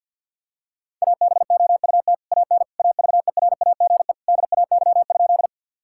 Morse code, heard as a keyed tone, about 50 wpm, 700 Hz. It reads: ABOUT AN AVERAGE DAY?
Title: Sentences from Top 1000 Words - Review Part 6 - 50wpm